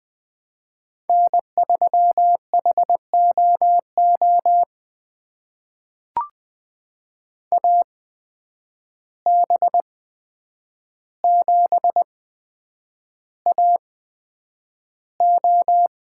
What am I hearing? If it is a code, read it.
Morse code, heard as a keyed tone, about 21 wpm, spaced as for 9 wpm, 700 Hz. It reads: N3HOO AB7AO
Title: USA Call Signs - 5 Characters - 8x Character Spacing - 20wpm